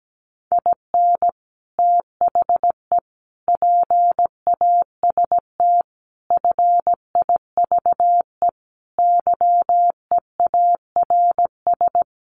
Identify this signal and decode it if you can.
Morse code, heard as a keyed tone, about 17 wpm, 700 Hz. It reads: IN THE PAST FIVE YEARS
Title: Sets of 5 Words Encore 17wpm